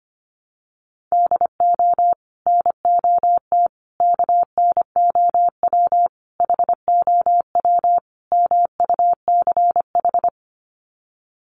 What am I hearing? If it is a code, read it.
Morse code, heard as a keyed tone, about 25 wpm, 700 Hz. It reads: DO NOT KNOW HOW MUCH
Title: Sets of 5 Words Encore 25wpm